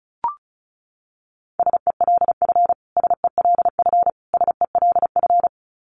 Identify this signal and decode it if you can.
Morse code, heard as a keyed tone, about 34 wpm, 700 Hz. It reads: SELF SELF SELF
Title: Top 500 Words Repeated 35wpm